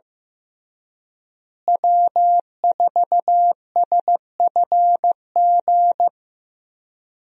Morse code, heard as a keyed tone, about 15 wpm, 700 Hz. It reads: W4SFG
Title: USA Call Signs - 5 Characters - Repeated 2x - 15wpm